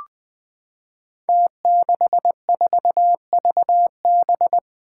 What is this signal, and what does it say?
Morse code, heard as a keyed tone, about 20 wpm, 700 Hz. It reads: T64VB
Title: Five Challenging Letter-Number Combinations 20wpm